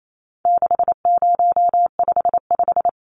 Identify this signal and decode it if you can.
Morse code, heard as a keyed tone, about 28 wpm, 700 Hz. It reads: 6055
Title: Numbers - 4-Digit 28wpm